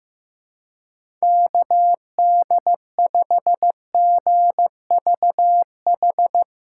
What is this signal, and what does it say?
Morse code, heard as a keyed tone, about 15 wpm, 700 Hz. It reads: KD5GVH
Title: Speed Racing - USA 2x3 Call Signs 15wpm